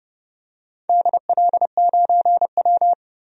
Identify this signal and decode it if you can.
Morse code, heard as a keyed tone, about 30 wpm, 700 Hz. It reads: DL9W
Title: DXCC-100 Call Signs - 4 Characters - 30wpm